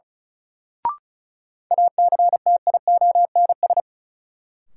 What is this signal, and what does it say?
Morse code, heard as a keyed tone, about 36 wpm, 700 Hz. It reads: ACTIONS
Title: Sentences from Top 1000 Words - Review Part 6 - New Word Drill - 35wpm